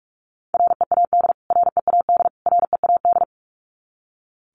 Morse code, heard as a keyed tone, about 44 wpm, 700 Hz. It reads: READ READ READ
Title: Top 300 Words Repeated 45wpm